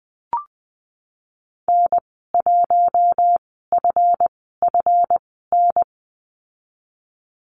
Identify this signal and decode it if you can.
Morse code, heard as a keyed tone, about 20 wpm, spaced as for 14 wpm, 700 Hz. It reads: N1FFN
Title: 140 Lesson 23g - Letter f - Call Signs 20wpm